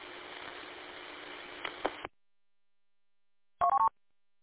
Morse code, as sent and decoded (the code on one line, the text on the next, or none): none